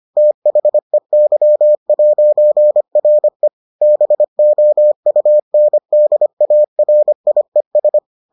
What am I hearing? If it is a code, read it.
Morse code, heard as a keyed tone, about 25 wpm, 600 Hz. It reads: THEY'RE BOUNDARIES